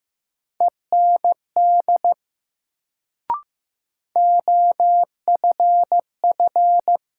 Morse code, heard as a keyed tone, about 15 wpm, 700 Hz. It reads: END OFF
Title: Sentences from Top 200 Words - New Word Drill - 15wpm